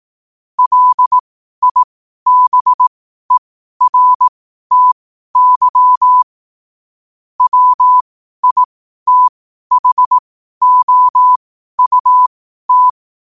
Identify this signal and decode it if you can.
Morse code, heard as a keyed tone, about 18 wpm, spaced as for 12 wpm, 1000 Hz. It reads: LIBERTY WITHOUT
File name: qotdfiles / 12wpm